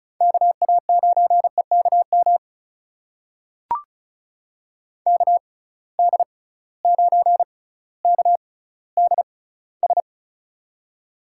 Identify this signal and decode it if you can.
Morse code, heard as a keyed tone, about 36 wpm, spaced as for 18 wpm, 690 Hz. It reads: KA9EKM KD9KDS
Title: USA Call Signs - 6 Characters - 6x Character Spacing - 35wpm